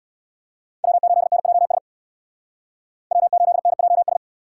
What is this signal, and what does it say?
Morse code, heard as a keyed tone, about 59 wpm, 700 Hz. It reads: W8NJS W8NJS
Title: Speed Racing - USA 1x3 Call Signs 50wpm